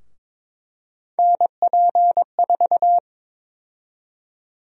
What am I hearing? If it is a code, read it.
Morse code, heard as a keyed tone, about 21 wpm, 700 Hz. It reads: NP4
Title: USA Call Sign Prefixes - 22wpm